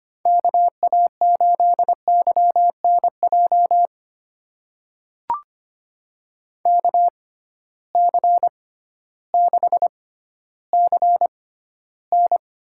Morse code, heard as a keyed tone, about 25 wpm, spaced as for 13 wpm, 700 Hz. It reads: KA8YNJ KC6CN
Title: USA Call Signs - 6 Characters - 6x Character Spacing - 25wpm